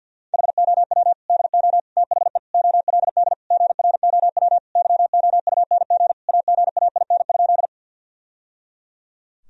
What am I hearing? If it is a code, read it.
Morse code, heard as a keyed tone, about 50 wpm, 700 Hz. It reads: HOW DO THE OLD GROW YOUNG AGAIN?
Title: Sentences from Top 1000 Words - Review Part 1 - 50wpm